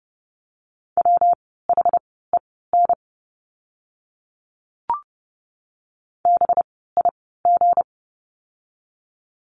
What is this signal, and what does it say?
Morse code, heard as a keyed tone, about 30 wpm, spaced as for 17 wpm, 700 Hz. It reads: WHEN BIG
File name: course / 30:10wpm / 237 Lesson 38f - Letter q - Words 30wpm